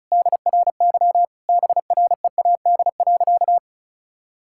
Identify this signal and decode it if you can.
Morse code, heard as a keyed tone, about 34 wpm, 700 Hz. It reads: DRY BREAD.
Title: Sentences from Top 1600 Words 35wpm